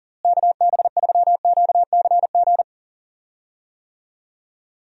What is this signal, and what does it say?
Morse code, heard as a keyed tone, about 40 wpm, 700 Hz. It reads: KB3QCG
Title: USA Call Signs - 6 Characters - Repeated 3x - 40wpm